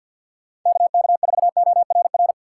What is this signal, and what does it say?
Morse code, heard as a keyed tone, about 53 wpm, 700 Hz. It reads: KK4ORR
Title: USA Call Signs - 6 Characters - 4x Character Spacing - 50wpm